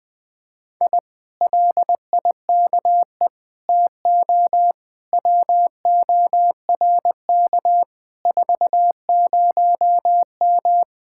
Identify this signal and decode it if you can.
Morse code, heard as a keyed tone, about 20 wpm, 700 Hz. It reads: I LIKE TO WORK 40M
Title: Speed-Racing Common QSO Phrases - 17wpm